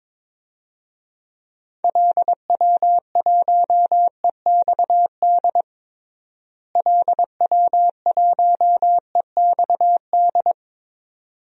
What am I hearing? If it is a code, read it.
Morse code, heard as a keyed tone, about 22 wpm, 700 Hz. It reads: LW1EXD LW1EXD
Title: DXCC-100 Call Signs - 6 Characters - Repeated 2x - 22wpm